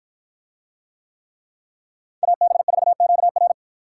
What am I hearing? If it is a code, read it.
Morse code, heard as a keyed tone, about 48 wpm, 700 Hz. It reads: AB4QR